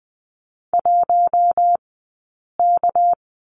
Morse code, heard as a keyed tone, about 20 wpm, spaced as for 8 wpm, 700 Hz. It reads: 1K
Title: 211 Lesson 34d - Number 8 - Two Characters 20wpm